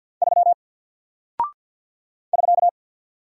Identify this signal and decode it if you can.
Morse code, heard as a keyed tone, about 48 wpm, 700 Hz. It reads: 3 <SK>
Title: Single Character (Letter:Number:Punctuation:Prosign) Alt 50wpm